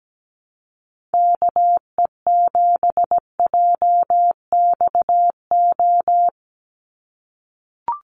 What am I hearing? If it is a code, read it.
Morse code, heard as a keyed tone, about 17 wpm, 700 Hz. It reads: KE7JXO
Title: USA Call Signs 17wpm